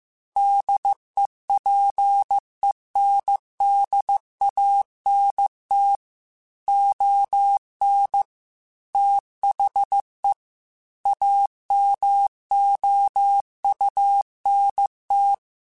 Morse code, heard as a keyed tone, about 15 wpm, 800 Hz. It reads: DEPENDANT ON THE AMOUNT